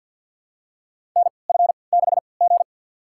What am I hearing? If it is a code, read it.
Morse code, heard as a keyed tone, about 53 wpm, spaced as for 29 wpm, 700 Hz. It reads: NF6G
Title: USA Call Signs - 4 Characters - 3x Character Spacing - 50wpm